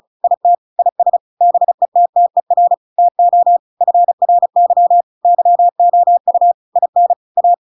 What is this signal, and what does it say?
Morse code, heard as a keyed tone, about 34 wpm, 700 Hz. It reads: IT IS BETTER TO FRY YOU IN A